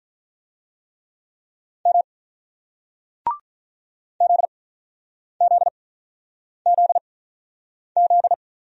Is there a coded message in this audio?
Yes: Morse code from 1.9 to 8.3 s, heard as a keyed tone, about 48 wpm, spaced as for 9 wpm, 700 Hz. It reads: M ZZZZ